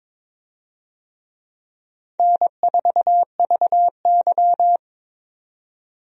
Morse code, heard as a keyed tone, about 22 wpm, 700 Hz. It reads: N4VY
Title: USA Call Signs - 4 Characters - 22wpm